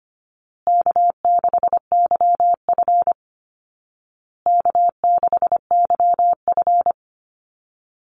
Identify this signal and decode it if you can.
Morse code, heard as a keyed tone, about 25 wpm, 700 Hz. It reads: K6YF K6YF
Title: Easy Call Signs Repeated 3x - 25wpm